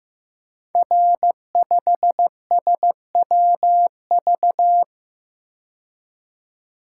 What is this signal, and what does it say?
Morse code, heard as a keyed tone, about 15 wpm, 700 Hz. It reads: R5SWV
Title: Code Groups - 5 Letter:Numbers - 15wpm